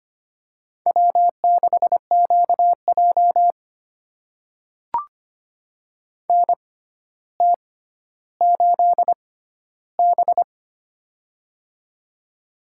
Morse code, heard as a keyed tone, about 25 wpm, spaced as for 13 wpm, 700 Hz. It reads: W6QJ NT8B
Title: USA Call Signs - 4 Characters - 6x Character Spacing - 25wpm